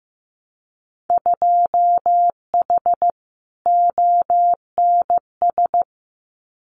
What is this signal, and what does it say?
Morse code, heard as a keyed tone, about 15 wpm, 700 Hz. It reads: 2H ONS
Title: Winter Field Day Exchanges Encore 15wpm